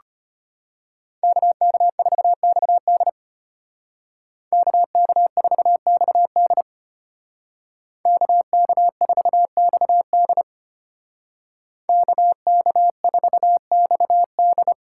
Morse code, heard as a keyed tone, about 32 wpm, 700 Hz. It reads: KK4XD KK4XD KK4XD KK4XD